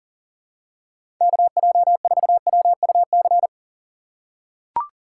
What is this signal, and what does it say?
Morse code, heard as a keyed tone, about 40 wpm, 700 Hz. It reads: KJ4WUC
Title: USA Call Signs - 6 Characters - 2x Character Spacing - 40wpm